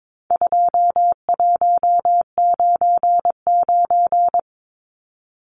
Morse code, heard as a keyed tone, about 22 wpm, 700 Hz. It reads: 2199
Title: Numbers - 4-Digit 22wpm